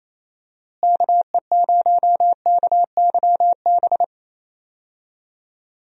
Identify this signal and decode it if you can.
Morse code, heard as a keyed tone, about 28 wpm, 700 Hz. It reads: KE0KYB
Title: USA Call Signs - 6 Characters - Repeated 3x - 28wpm